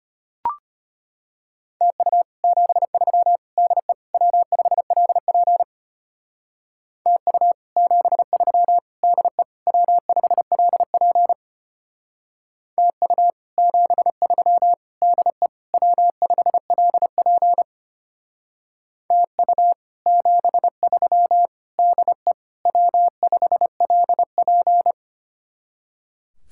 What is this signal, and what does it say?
Morse code, heard as a keyed tone, about 33 wpm, 700 Hz. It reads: TU 73 DE W5LP TU 73 DE W5LP TU 73 DE W5LP TU 73 DE W5LP